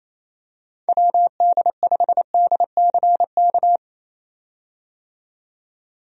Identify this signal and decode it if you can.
Morse code, heard as a keyed tone, about 28 wpm, 700 Hz. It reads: WD5DCK
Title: USA Call Signs - 6 Characters - 28wpm